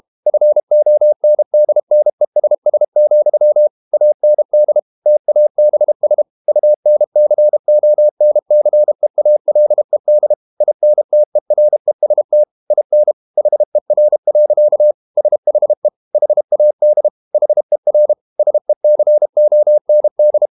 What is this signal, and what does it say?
Morse code, heard as a keyed tone, about 32 wpm, 600 Hz. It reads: FONDNESS, AND TABS UNCONCEALED INTEREST IN HER. SHE HAD HER SECOND
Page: Never 31wpm | CW med Gnister